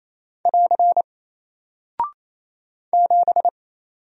Morse code, heard as a keyed tone, about 28 wpm, 700 Hz. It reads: <AR> 7
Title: Single Character (Letter:Number:Punctuation:Prosign) Alt 28wpm